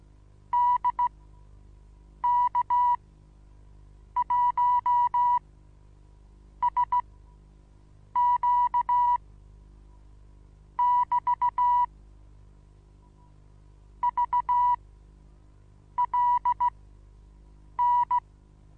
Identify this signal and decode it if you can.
Morse code, heard as a keyed tone, about 16 wpm, spaced as for 6 wpm, 970 Hz. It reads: DK1SQ <BT> VLN
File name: Onlinelehrgang / morsen / DJ4UF Morsen25